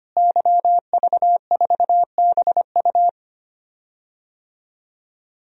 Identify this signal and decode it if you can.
Morse code, heard as a keyed tone, about 25 wpm, 700 Hz. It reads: YV4BU
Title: DXCC-100 Call Signs - 5 Characters - Repeated 3x - 25wpm